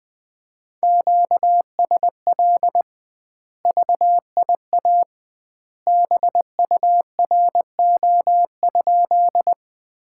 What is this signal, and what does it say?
Morse code, heard as a keyed tone, about 20 wpm, 700 Hz. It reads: QSL VIA BURO?
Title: Common QSO Phrases - 2x Word-Spacing - 20wpm